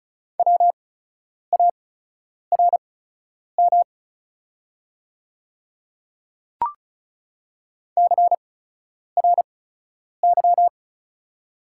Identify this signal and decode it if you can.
Morse code, heard as a keyed tone, about 36 wpm, spaced as for 10 wpm, 690 Hz. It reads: WARM CRY
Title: 8x Character Spacing - Top 500 Words - 35wpm